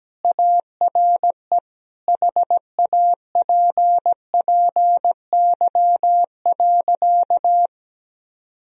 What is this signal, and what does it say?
Morse code, heard as a keyed tone, about 17 wpm, 700 Hz. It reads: ARE HAPPY.
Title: Sentences from Top 1000 Words - Review Part 3 - 17wpm